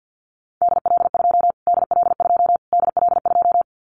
Morse code, heard as a keyed tone, about 48 wpm, 700 Hz. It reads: DL2 DL2 DL2
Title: Callsign Prefixes for Top 25 DXCC - Repeated 3x - 50wpm